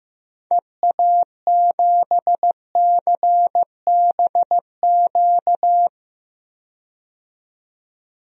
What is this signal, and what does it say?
Morse code, heard as a keyed tone, about 15 wpm, 700 Hz. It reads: EA7CBQ
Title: DXCC-100 Call Signs - 6 Characters - 15wpm